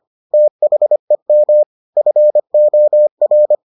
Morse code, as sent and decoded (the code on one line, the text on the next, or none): - .... -- / ..-. --- .-.
THEM FOR